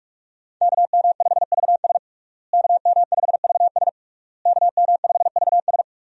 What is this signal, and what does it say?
Morse code, heard as a keyed tone, about 45 wpm, 700 Hz. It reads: KM5VS KM5VS KM5VS
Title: USA Call Signs - 5 Characters - Repeated 3x - 45wpm